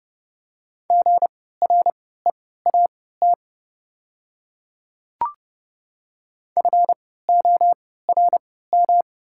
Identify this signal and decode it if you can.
Morse code, heard as a keyed tone, about 30 wpm, spaced as for 17 wpm, 700 Hz. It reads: GREAT FORM
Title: 3x Character Spacing - Top 300 Words - 30wpm